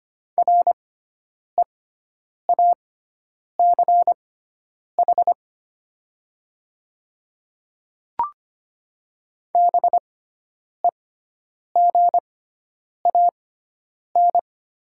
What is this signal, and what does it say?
Morse code, heard as a keyed tone, about 25 wpm, spaced as for 9 wpm, 700 Hz. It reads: REACH BEGAN
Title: 6x Character Spacing - Top 500 Words - 25wpm